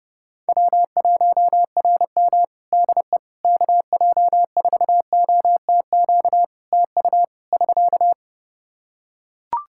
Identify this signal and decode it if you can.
Morse code, heard as a keyed tone, about 30 wpm, 700 Hz. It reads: W1RM DE KJ4OTQ TU <SK>
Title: Practice QSO - Exchange 4 - Ending - 30wpm